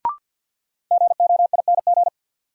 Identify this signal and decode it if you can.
Morse code, heard as a keyed tone, about 52 wpm, 700 Hz. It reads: GOING
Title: Sets of 2 Words - New Word Drill - 50wpm